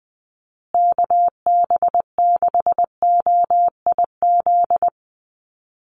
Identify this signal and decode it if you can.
Morse code, heard as a keyed tone, about 20 wpm, 700 Hz. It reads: KB6OIZ
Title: USA Call Signs Repeated 2x 20wpm